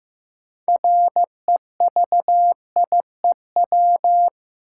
Morse code, heard as a keyed tone, about 15 wpm, 700 Hz. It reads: REVIEW
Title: Sentences from Top 900 Words - New Word Drill - 15wpm